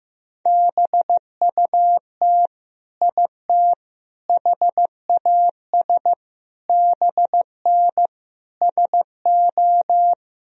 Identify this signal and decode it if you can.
Morse code, heard as a keyed tone, about 15 wpm, 700 Hz. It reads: BUT IT HAS BN SO
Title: Sentences from Top 100 Words and Abbreviations 15wpm